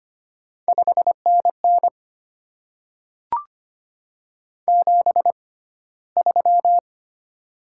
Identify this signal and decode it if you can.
Morse code, heard as a keyed tone, about 25 wpm, spaced as for 15 wpm, 700 Hz. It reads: 5NN 73
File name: Top 20 QSO Elements - 6x Character Spacing - 25wpm